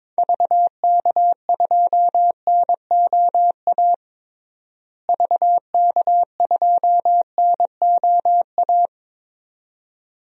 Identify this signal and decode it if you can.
Morse code, heard as a keyed tone, about 22 wpm, 700 Hz. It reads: VK2NOA VK2NOA
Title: DXCC-100 Call Signs - 6 Characters - Repeated 2x - 22wpm